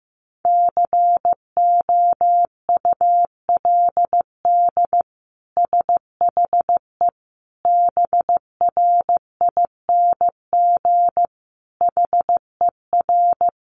Morse code, heard as a keyed tone, about 15 wpm, 700 Hz. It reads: COULD SHE BRING HER
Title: Sentences from Top 1400 Words 15wpm